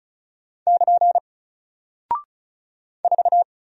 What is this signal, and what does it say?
Morse code, heard as a keyed tone, about 36 wpm, 700 Hz. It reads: <KN> 4
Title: Single Character (Letter:Number:Punctuation:Prosign) 35wpm